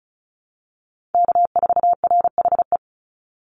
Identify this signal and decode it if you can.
Morse code, heard as a keyed tone, about 36 wpm, 700 Hz. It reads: K4RHE
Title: USA Call Signs 35wpm